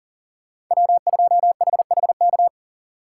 Morse code, heard as a keyed tone, about 40 wpm, 700 Hz. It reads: W2HHK
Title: USA Call Signs - 5 Characters - 3x Character Spacing - 40wpm